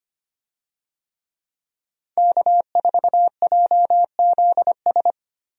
Morse code, heard as a keyed tone, about 25 wpm, 700 Hz. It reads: K4JZS
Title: USA Call Signs - 5 Characters - 25wpm